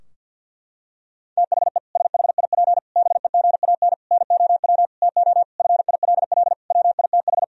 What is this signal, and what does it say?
Morse code, heard as a keyed tone, about 51 wpm, 700 Hz. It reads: THE SHIP BEGAN NOW TO FILL WITH
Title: Sentences from Top 1000 Words - Review Part 3 - 50wpm